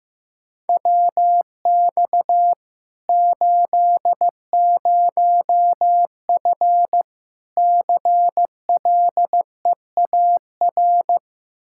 Morse code, heard as a keyed tone, about 15 wpm, 700 Hz. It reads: WX 80F CLEAR
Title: Practice QSO - Exchange 2 - Info - 15wpm